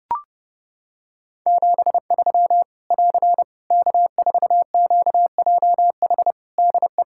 Morse code, heard as a keyed tone, about 30 wpm, 700 Hz. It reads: 73 <AR> K4QJH DE
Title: Practice QSO - Exchange 4 - Ending - 30wpm